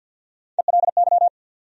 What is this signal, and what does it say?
Morse code, heard as a keyed tone, about 53 wpm, 700 Hz. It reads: ELY
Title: USA Call Sign Postfixes - 3 Letters 50wpm